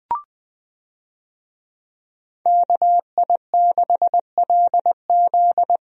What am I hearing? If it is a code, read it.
Morse code, heard as a keyed tone, about 20 wpm, 700 Hz. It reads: KI6LZ
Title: USA Call Signs - 5 Characters - Repeated 3x - 20wpm